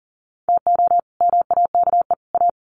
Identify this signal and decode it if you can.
Morse code, heard as a keyed tone, about 40 wpm, 700 Hz. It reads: TO MAKE A
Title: Sets of 3 Words 40wpm